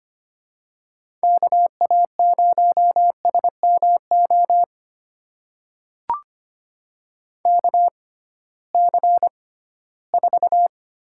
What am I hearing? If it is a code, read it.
Morse code, heard as a keyed tone, about 25 wpm, spaced as for 16 wpm, 700 Hz. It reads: KA0SMO KC4
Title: USA Call Signs - 6 Characters - 6x Character Spacing - 25wpm